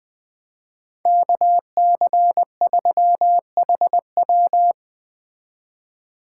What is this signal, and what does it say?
Morse code, heard as a keyed tone, about 20 wpm, 700 Hz. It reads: KC3HW
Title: Speed Racing - USA 2x2 Call Signs 17wpm